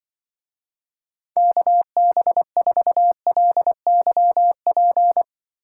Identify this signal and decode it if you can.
Morse code, heard as a keyed tone, about 24 wpm, 700 Hz. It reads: KB4LYP
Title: Speed Racing - USA 2x3 Call Signs 20wpm